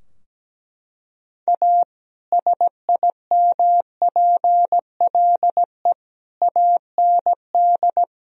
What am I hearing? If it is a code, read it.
Morse code, heard as a keyed tone, about 17 wpm, 700 Hz. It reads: A SIMPLE AND
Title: Sentences from Top 1000 Words - Review Part 4 - 17wpm